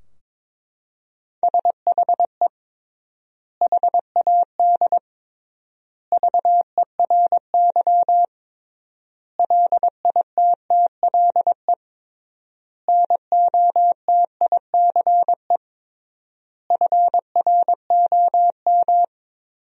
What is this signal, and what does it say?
Morse code, heard as a keyed tone, about 22 wpm, 700 Hz. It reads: SHE HAD VERY LITTLE NOTICE FROM
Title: 3x Word-Spacing - Sentences from Top 400 Words - 22wpm